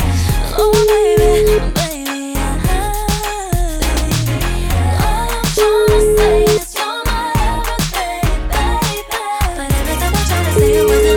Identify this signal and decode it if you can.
Ringback tone (Europe, 425 Hz), with noise.